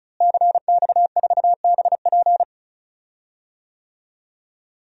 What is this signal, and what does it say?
Morse code, heard as a keyed tone, about 35 wpm, 700 Hz. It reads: CX4BP
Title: DXCC-100 Call Signs - 5 Characters - Repeated 3x - 35wpm